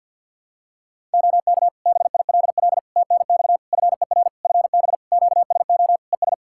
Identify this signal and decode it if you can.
Morse code, heard as a keyed tone, about 51 wpm, 700 Hz. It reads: OK BILL TNX FER FB QSO ES
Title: Practice QSO - Exchange 4 - Ending - 50wpm